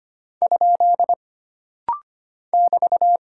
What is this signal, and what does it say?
Morse code, heard as a keyed tone, about 25 wpm, 700 Hz. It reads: ? <BT>
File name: Single Punctuation-Prosign - Rapid-Fire - 25wpm